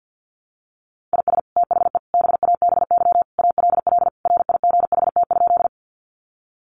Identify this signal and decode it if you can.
Morse code, heard as a keyed tone, about 50 wpm, 700 Hz. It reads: IS THE BABY ALL RIGHT?
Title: Sentences from Top 1300 Words 50wpm